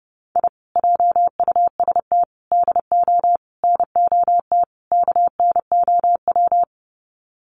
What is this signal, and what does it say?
Morse code, heard as a keyed tone, about 30 wpm, 700 Hz. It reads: I JUST DO NOT KNOW
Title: Sets of 5 Words Encore 30wpm